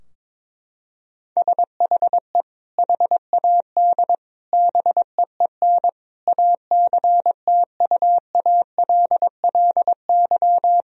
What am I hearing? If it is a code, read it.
Morse code, heard as a keyed tone, about 22 wpm, 700 Hz. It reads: SHE HAD BEEN ACTUALLY
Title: Sentences from Top 1000 Words - Review Part 4 - 22wpm